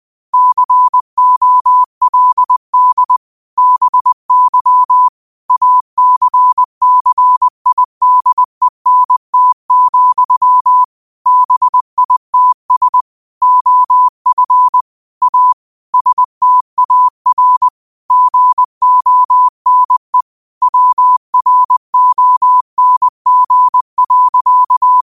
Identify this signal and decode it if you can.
Morse code, heard as a keyed tone, about 20 wpm, 1000 Hz. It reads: COLD BY ACCIDENT, BITS OF A STAR GONE WRONG.
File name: qotdfiles / 20wpm